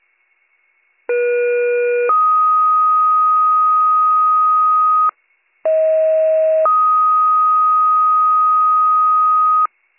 Two-tone page, heard as a one-second tone, then a three-second tone, about 480 then 1250 Hz, about 640 then 1250 Hz. It is heard from 1.1 to 9.7 s.